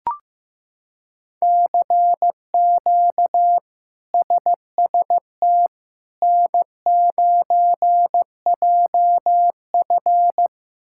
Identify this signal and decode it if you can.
Morse code, heard as a keyed tone, about 15 wpm, 700 Hz. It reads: CQ SST N9JF
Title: K1USN SST QSO Practice - 15wpm